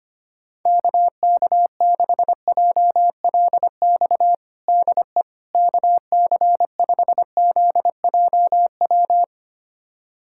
Morse code, heard as a keyed tone, about 25 wpm, 700 Hz. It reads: KK6JLX DE KC5ZJW
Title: USA Callsign DE Callsign 25wpm